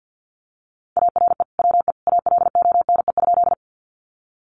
Morse code, heard as a keyed tone, about 50 wpm, 700 Hz. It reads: ARE WE ALONE?
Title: Sentences from Top 1000 Words 50wpm